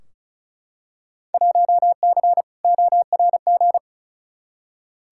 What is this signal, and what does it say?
Morse code, heard as a keyed tone, about 36 wpm, 700 Hz. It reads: 1C ORG
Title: Field Day Exchanges - 35wpm